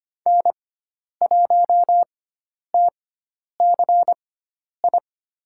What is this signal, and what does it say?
Morse code, heard as a keyed tone, about 25 wpm, spaced as for 10 wpm, 700 Hz. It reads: N1TCI
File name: Lesson 16g - Letter c - Call Signs 25wpm